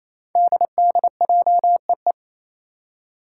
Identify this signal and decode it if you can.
Morse code, heard as a keyed tone, about 27 wpm, 700 Hz. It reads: DDJEE
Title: Five Challenging Letter-Number Combinations 28wpm